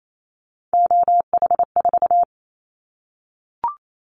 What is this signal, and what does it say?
Morse code, heard as a keyed tone, about 28 wpm, 700 Hz. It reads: OH4
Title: Callsign Prefixes for Top 50 DXCC - 28wpm